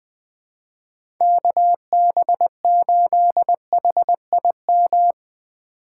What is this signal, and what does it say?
Morse code, heard as a keyed tone, about 20 wpm, 700 Hz. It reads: KB8HIM
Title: USA Call Signs - 4x Character Spacing - 20wpm